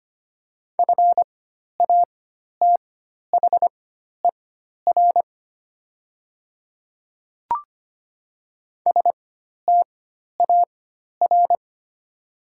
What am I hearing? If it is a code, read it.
Morse code, heard as a keyed tone, about 25 wpm, spaced as for 12 wpm, 700 Hz. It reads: FATHER STAR
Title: 4x Character Spacing - Top 500 Words - 25wpm